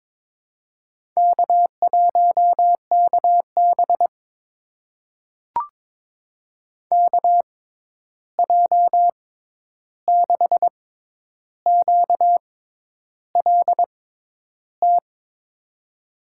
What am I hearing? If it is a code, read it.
Morse code, heard as a keyed tone, about 22 wpm, spaced as for 10 wpm, 700 Hz. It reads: K1KB KJ6QLT
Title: USA Call Signs - 6x Character Spacing - 22wpm